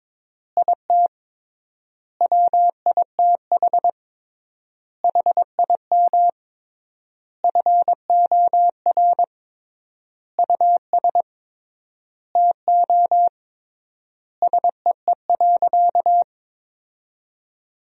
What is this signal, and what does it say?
Morse code, heard as a keyed tone, about 22 wpm, 700 Hz. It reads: IT WITH HIM FOR US TO SEE.
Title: 3x Word-Spacing - Sentences from Top 500 Words - 22wpm